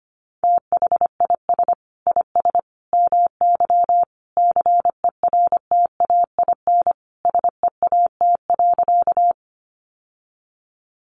Morse code, heard as a keyed tone, about 25 wpm, 700 Hz. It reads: THIS IS MY CERTAIN SEAT.
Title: Sentences from Top 1500 Words 25wpm